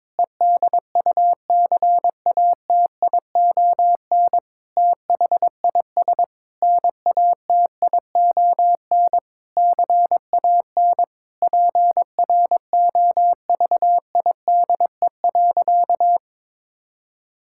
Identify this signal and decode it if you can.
Morse code, heard as a keyed tone, about 22 wpm, 700 Hz. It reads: EDUCATION THIS NATION CAN PROVIDE.
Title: Sentences from Top 1000 Words - Review Part 6 - 22wpm